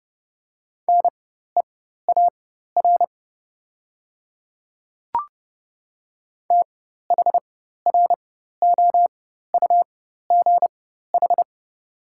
Morse code, heard as a keyed tone, about 30 wpm, spaced as for 14 wpm, 700 Hz. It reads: NEAR THROUGH
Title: 4x Character Spacing - Top 300 Words - 30wpm